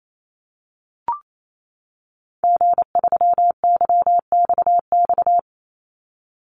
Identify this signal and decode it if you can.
Morse code, heard as a keyed tone, about 28 wpm, 700 Hz. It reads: G3YXX
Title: Call Signs - 28wpm